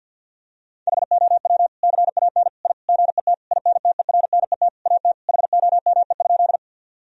Morse code, heard as a keyed tone, about 50 wpm, 700 Hz. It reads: HOW CAN I GET INTERNET AT HOME?
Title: Sentences from Top 1000 Words - Review Part 4 - 50wpm